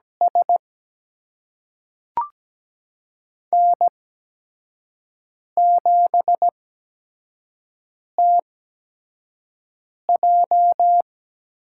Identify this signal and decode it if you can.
Morse code, heard as a keyed tone, about 17 wpm, spaced as for 5 wpm, 700 Hz. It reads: S N7TJ
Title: USA Call Signs - 4 Characters - 8x Character Spacing - 17wpm